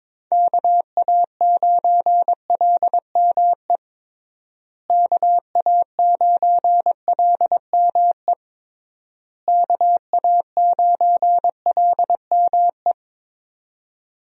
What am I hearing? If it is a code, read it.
Morse code, heard as a keyed tone, about 22 wpm, 700 Hz. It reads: KA9LME KA9LME KA9LME
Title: USA Call Signs - 6 Characters - Repeated 3x - 22wpm